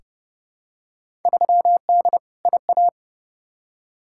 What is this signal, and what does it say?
Morse code, heard as a keyed tone, about 30 wpm, 700 Hz. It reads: 3D IA
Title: Field Day Exchanges Encore - 30wpm